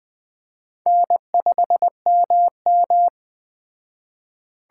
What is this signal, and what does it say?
Morse code, heard as a keyed tone, about 20 wpm, 700 Hz. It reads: N5MM
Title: Speed Racing - USA 1x2 Call Signs 17wpm